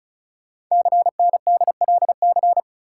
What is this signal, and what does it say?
Morse code, heard as a keyed tone, about 36 wpm, 690 Hz. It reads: CNDLC
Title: Code Groups - 5 Letters - 35wpm